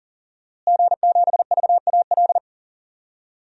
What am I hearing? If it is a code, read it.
Morse code, heard as a keyed tone, about 40 wpm, 700 Hz. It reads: G7VAL
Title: DXCC-100 Call Signs - 5 Characters - Repeated 2x - 40wpm